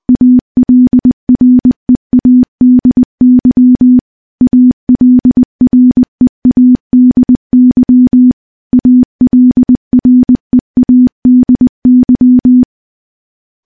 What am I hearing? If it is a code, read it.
Morse code, heard as a keyed tone, about 20 wpm, 250 Hz. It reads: ALREADY ALREADY ALREADY
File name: morse-training-20WPM